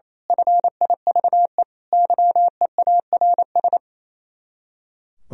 Morse code, heard as a keyed tone, about 26 wpm, 700 Hz. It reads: FIVE YEARS
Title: Speed-Racing - Sets of 5 Words 28wpm